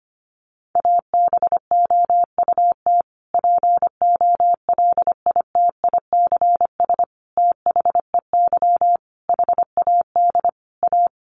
Morse code, heard as a keyed tone, about 25 wpm, 700 Hz. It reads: ABOUT POLITICS THEY HAD A
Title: Sentences from Top 1000 Words 25wpm